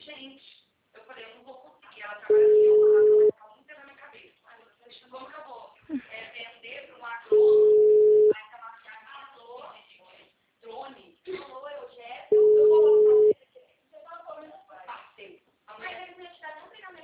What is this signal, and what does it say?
Ringback tone (Europe, 425 Hz), with speech.